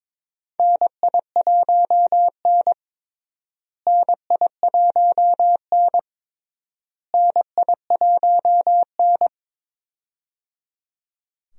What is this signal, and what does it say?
Morse code, heard as a keyed tone, about 21 wpm, 700 Hz. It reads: NI1N NI1N NI1N
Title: USA Call Signs - 4 Characters - Repeated 3x - 22wpm